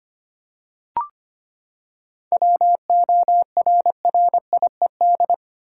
Morse code, heard as a keyed tone, about 25 wpm, 700 Hz. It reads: WORRIED
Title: Seven Letter Words - 25wpm